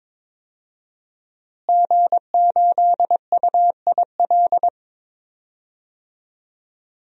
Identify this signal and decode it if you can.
Morse code, heard as a keyed tone, about 21 wpm, 700 Hz. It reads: G8UIL
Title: DXCC-100 Call Signs - 5 Characters - Repeated 3x - 22wpm